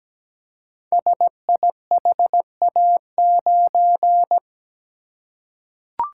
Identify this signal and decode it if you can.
Morse code, heard as a keyed tone, about 17 wpm, 700 Hz. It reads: SIHA9
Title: Code Groups - 5 Letter:Numbers - 17wpm